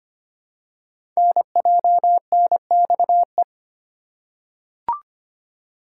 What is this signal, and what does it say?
Morse code, heard as a keyed tone, about 25 wpm, 700 Hz. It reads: NJNXE
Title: Code Groups - 5 Letters - 25wpm